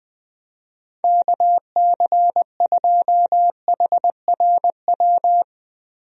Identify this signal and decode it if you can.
Morse code, heard as a keyed tone, about 20 wpm, 700 Hz. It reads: KC2HRW
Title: USA Call Signs - 6 Characters - Repeated 3x - 20wpm